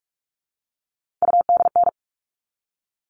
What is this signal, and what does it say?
Morse code, heard as a keyed tone, about 44 wpm, 700 Hz. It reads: UDN
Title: All Cut Numbers - 3-Digit 45wpm